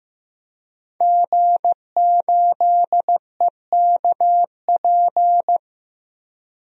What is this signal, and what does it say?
Morse code, heard as a keyed tone, about 15 wpm, 700 Hz. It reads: G8EKP
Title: DXCC-100 Call Signs - 5 Characters - Repeated 3x - 15wpm